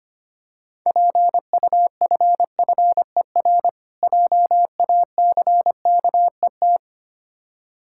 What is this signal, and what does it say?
Morse code, heard as a keyed tone, about 25 wpm, 690 Hz. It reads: PUFFER JACKET
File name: practice / 25wpm / Difficult Words with Hints - 25wpm